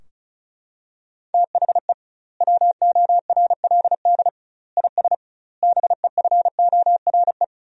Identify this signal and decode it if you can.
Morse code, heard as a keyed tone, about 36 wpm, 700 Hz. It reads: THE WORLD IS BEFORE
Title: 2x Word-Spacing - Sentences from Top 200 Words - 35wpm